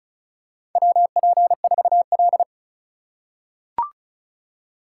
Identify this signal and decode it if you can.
Morse code, heard as a keyed tone, about 36 wpm, 700 Hz. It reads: WP4L
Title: USA Call Signs - 4 Characters - 6x Character Spacing - 35wpm